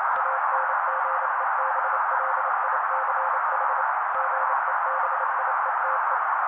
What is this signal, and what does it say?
Morse code, heard as a keyed tone, about 29 wpm, 530 Hz. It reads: ANGEBLICH GEBIETE
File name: daily / 27wpm noise